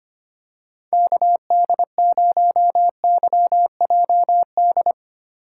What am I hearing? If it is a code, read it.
Morse code, heard as a keyed tone, about 25 wpm, 700 Hz. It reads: KD0YJD
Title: USA Call Signs - 6 Characters - 4x Character Spacing - 25wpm